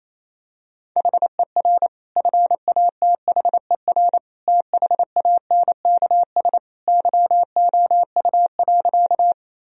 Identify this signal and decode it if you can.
Morse code, heard as a keyed tone, about 28 wpm, 700 Hz. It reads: HER FATHER THANKS YOU.
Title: Sentences from Top 1000 Words - Review Part 3 - 28wpm